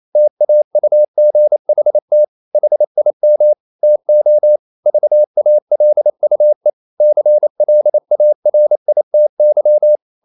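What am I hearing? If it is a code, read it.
Morse code, heard as a keyed tone, about 28 wpm, 600 Hz. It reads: TAUGHT HIM TO VALUE CLARITY
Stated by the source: Morse code practice at 28 wpm, text of Ken Follett's novel Never in English, from CW med Gnister.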